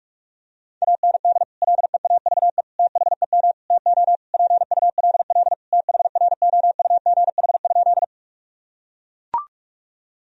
Morse code, heard as a keyed tone, about 41 wpm, 700 Hz. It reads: AND LEAVE THEM TO PULL THROUGH?
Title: Sentences from Top 1000 Words - Review Part 2 - 45wpm